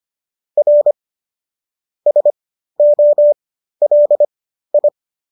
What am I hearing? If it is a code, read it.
Morse code, heard as a keyed tone, about 25 wpm, spaced as for 13 wpm, 590 Hz. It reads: R SOLI